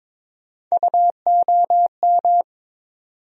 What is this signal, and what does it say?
Morse code, heard as a keyed tone, about 22 wpm, 700 Hz. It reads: UOM
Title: USA Call Sign Postfixes - 3 Letters 22wpm